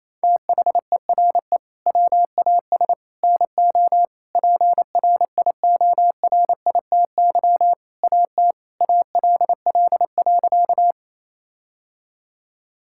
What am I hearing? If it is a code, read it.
Morse code, heard as a keyed tone, about 28 wpm, 700 Hz. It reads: THERE WAS NO PRIORITY AT ALL.
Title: Sentences from Top 1600 Words 28wpm